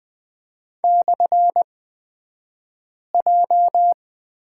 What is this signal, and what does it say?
Morse code, heard as a keyed tone, about 20 wpm, spaced as for 5 wpm, 700 Hz. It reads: /J
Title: 242 Lesson 39d - Letter z - Two Characters 20wpm